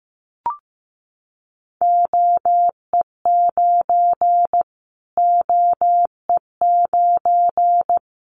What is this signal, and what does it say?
Morse code, heard as a keyed tone, about 15 wpm, 700 Hz. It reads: OE9 OE9
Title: Callsign Prefixes for Top 25 DXCC - Repeated 3x - 15wpm